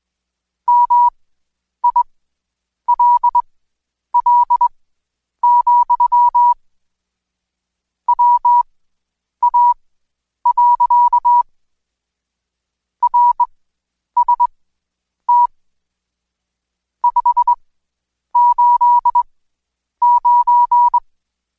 Morse code, heard as a keyed tone, about 21 wpm, spaced as for 9 wpm, 970 Hz. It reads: MILL, WA. RST 589
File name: Code Quick / AudioDownld2 / Seg115-Day20-DayDrill 20